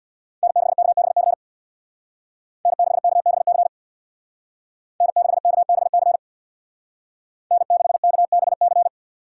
Morse code, heard as a keyed tone, about 58 wpm, 700 Hz. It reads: N6KBC N6KBC N6KBC N6KBC
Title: Speed Racing - USA 1x3 Call Signs 50wpm